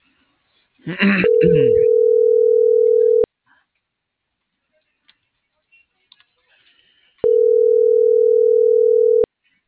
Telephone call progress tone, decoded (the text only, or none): ringback tone